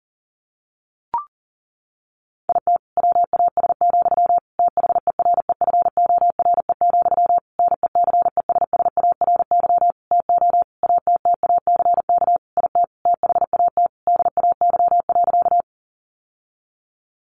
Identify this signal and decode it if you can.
Morse code, heard as a keyed tone, about 40 wpm, 700 Hz. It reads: IT WAS, THEREFORE, NECESSARY TO ATTACK IT THAT DAY.
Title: Sentences from Top 1500 Words 40wpm